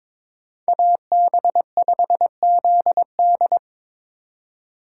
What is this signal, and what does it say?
Morse code, heard as a keyed tone, about 25 wpm, 700 Hz. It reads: AB5ZD
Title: Speed Racing - USA 2x2 Call Signs 22wpm